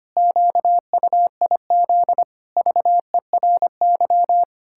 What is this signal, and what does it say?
Morse code, heard as a keyed tone, about 25 wpm, 700 Hz. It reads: QUIZ VERY
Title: Pangram Challenge v2 - 25wpm